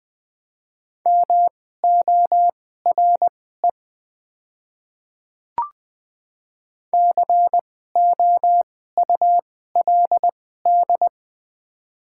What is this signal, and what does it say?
Morse code, heard as a keyed tone, about 20 wpm, spaced as for 15 wpm, 700 Hz. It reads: MORE COULD
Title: 2x Character Spacing - Top 100 Words - 20wpm